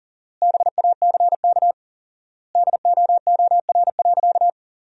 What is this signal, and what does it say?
Morse code, heard as a keyed tone, about 37 wpm, 700 Hz. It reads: BACK DOOR.
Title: 4x Word-Spacing - Sentences from Top 300 Words - 40wpm